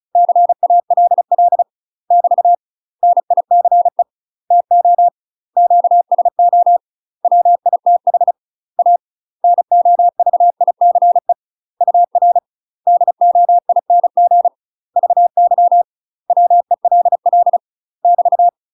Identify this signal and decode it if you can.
Morse code, heard as a keyed tone, about 35 wpm, 700 Hz. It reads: CALL <BT> NICE TO QSO WITH A NOVICE UR DOING VY WELL <BT>
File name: UK MASTER QSOs TRAINING / 35WPM / UK MASTER QSOs 0005